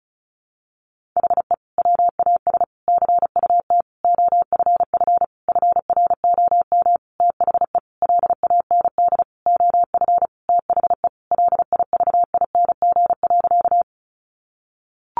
Morse code, heard as a keyed tone, about 34 wpm, 700 Hz. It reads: HE WAS CUT OFF FROM THE LAND OF THE LIVING.
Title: Sentences from Top 1100 Words 35wpm